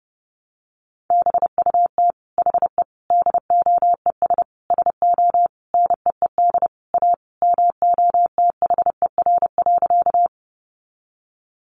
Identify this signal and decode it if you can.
Morse code, heard as a keyed tone, about 30 wpm, 700 Hz. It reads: BUT HE DOES SO NEED A MOTHER.
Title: Sentences from Top 200 Words 30wpm